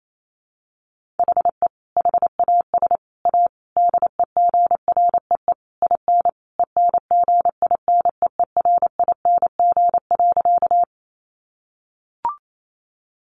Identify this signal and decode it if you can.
Morse code, heard as a keyed tone, about 28 wpm, 700 Hz. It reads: HE HAS A DEGREE IN ENGINEERING.
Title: Sentences from Top 1500 Words 28wpm